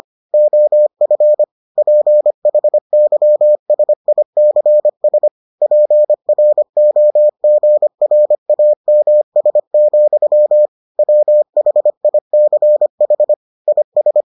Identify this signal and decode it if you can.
Morse code, heard as a keyed tone, about 25 wpm, 600 Hz. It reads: OF PHYSICS PROGRAMS, WHICH IS